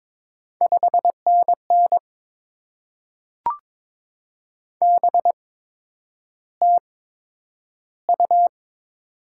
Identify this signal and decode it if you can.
Morse code, heard as a keyed tone, about 22 wpm, spaced as for 9 wpm, 700 Hz. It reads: 5NN BTU
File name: Top 20 QSO Elements - 8x Character Spacing - 22wpm